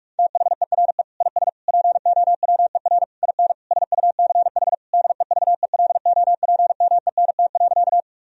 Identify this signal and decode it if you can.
Morse code, heard as a keyed tone, about 41 wpm, 700 Hz. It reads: THERE IS POWER IN SUCH DEVELOPMENT.
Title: Sentences from Top 1000 Words - Review Part 4 - 45wpm